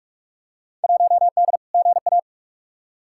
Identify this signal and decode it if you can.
Morse code, heard as a keyed tone, about 46 wpm, 700 Hz. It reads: 1D GA
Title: Field Day Exchanges - 45wpm